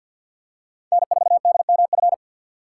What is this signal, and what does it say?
Morse code, heard as a keyed tone, about 48 wpm, 700 Hz. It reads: N4DMF
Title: USA Call Signs - 6x Character Spacing - 50wpm